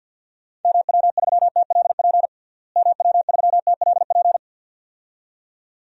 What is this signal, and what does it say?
Morse code, heard as a keyed tone, about 51 wpm, 700 Hz. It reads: MW3TLP MW3TLP
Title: DXCC-100 Call Signs - 6 Characters - Repeated 2x - 50wpm